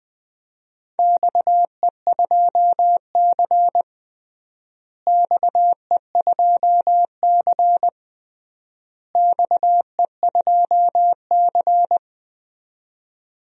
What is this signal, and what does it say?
Morse code, heard as a keyed tone, about 20 wpm, 700 Hz. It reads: XE2C XE2C XE2C
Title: DXCC-100 Call Signs - 4 Characters - Repeated 3x - 20wpm